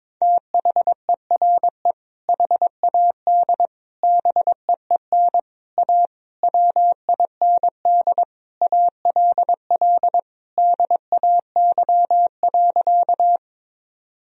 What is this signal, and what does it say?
Morse code, heard as a keyed tone, about 22 wpm, 700 Hz. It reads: THERE HAD BEEN A WIND ALL DAY.
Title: Sentences from Top 1000 Words - Review Part 2 - 22wpm